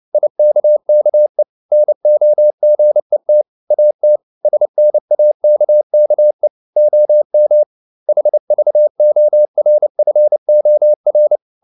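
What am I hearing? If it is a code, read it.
Morse code, heard as a keyed tone, about 29 wpm, 600 Hz. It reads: IKKE NOGET AT SNAKKE OM HVORFOR
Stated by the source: Fast (QRQ) Morse at 29 wpm, from CW med Gnister.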